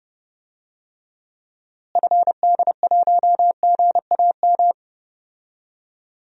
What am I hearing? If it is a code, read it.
Morse code, heard as a keyed tone, about 30 wpm, 700 Hz. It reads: FD1GAM
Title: DXCC-100 Call Signs - 6 Characters - Repeated 2x - 30wpm